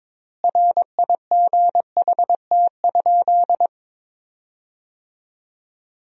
Morse code, heard as a keyed tone, about 22 wpm, 700 Hz. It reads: RIGHT?
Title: 4x Word-Spacing - Sentences from Top 200 Words - 22wpm